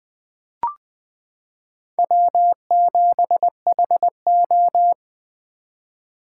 Morse code, heard as a keyed tone, about 20 wpm, 700 Hz. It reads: W7HO
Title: 2023 SOTA Chaser Call Signs - 20wpm